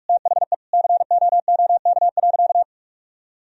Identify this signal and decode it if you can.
Morse code, heard as a keyed tone, about 41 wpm, 700 Hz. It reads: THE COOK.
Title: Sentences from Top 1000 Words - Review Part 4 - 45wpm